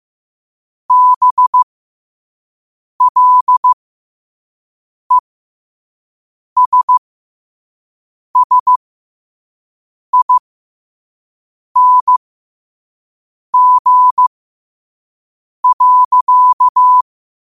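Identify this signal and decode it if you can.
Morse code, heard as a keyed tone, about 15 wpm, spaced as for 5 wpm, 1000 Hz. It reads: BLESSING.
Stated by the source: Morse code practice at 5 wpm.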